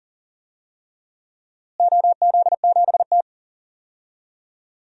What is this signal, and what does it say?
Morse code, heard as a keyed tone, about 40 wpm, 700 Hz. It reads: OZ7T
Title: DXCC-100 Call Signs - 4 Characters - Repeated 3x - 40wpm